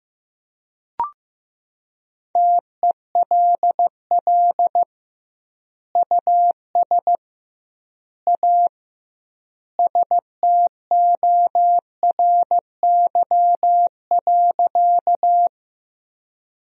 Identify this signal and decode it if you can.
Morse code, heard as a keyed tone, about 15 wpm, 700 Hz. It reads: TELL US A STORY.
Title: 2x Word-Spacing - Sentences from Top 300 Words - 15wpm